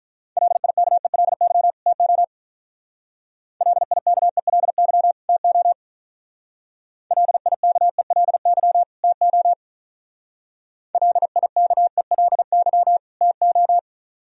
Morse code, heard as a keyed tone, about 42 wpm, 700 Hz. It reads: LIKELY TO LIKELY TO LIKELY TO LIKELY TO
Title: Speed-Racing - Sets of 2 Words 35wpm